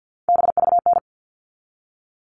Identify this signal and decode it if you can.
Morse code, heard as a keyed tone, about 50 wpm, 700 Hz. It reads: BVN
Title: All Cut Numbers - 3-Digit 50wpm